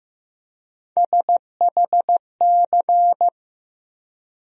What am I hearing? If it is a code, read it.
Morse code, heard as a keyed tone, about 15 wpm, 700 Hz. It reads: SHC
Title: USA Call Sign Postfixes - 3 Letters 15wpm